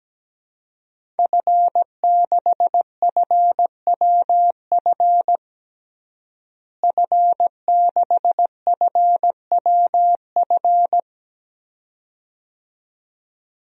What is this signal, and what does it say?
Morse code, heard as a keyed tone, about 17 wpm, 700 Hz. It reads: F6FWF F6FWF
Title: DXCC-100 Call Signs - 5 Characters - Repeated 2x - 17wpm